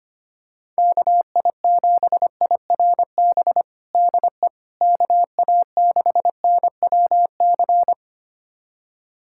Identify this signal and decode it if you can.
Morse code, heard as a keyed tone, about 25 wpm, 700 Hz. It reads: KI7IRB DE KA6NWC
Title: USA Callsign DE Callsign - 6 Characters - 25wpm